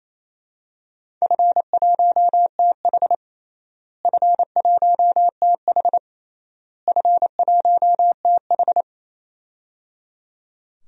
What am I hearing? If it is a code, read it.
Morse code, heard as a keyed tone, about 28 wpm, 700 Hz. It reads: F1TH F1TH F1TH
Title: DXCC-100 Call Signs - 4 Characters - Repeated 3x - 28wpm